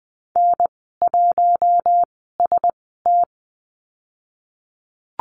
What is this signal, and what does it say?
Morse code, heard as a keyed tone, about 20 wpm, spaced as for 15 wpm, 700 Hz. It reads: N1ST
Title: 048 Lesson 9g - Number 4 - Call Signs 20wpm